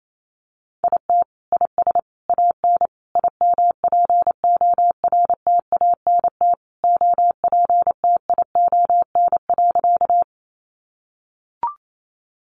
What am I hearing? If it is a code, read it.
Morse code, heard as a keyed tone, about 28 wpm, 700 Hz. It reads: IT IS AN IMPORTANT OPTION.
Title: Sentences from Top 1300 Words 28wpm